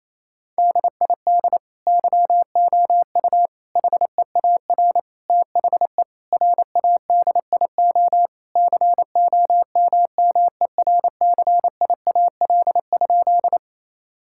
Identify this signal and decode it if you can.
Morse code, heard as a keyed tone, about 28 wpm, 700 Hz. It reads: DID YOU HEAR THE RADIO COMMERCIAL?
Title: Sentences from Top 1000 Words - Review Part 5 - 28wpm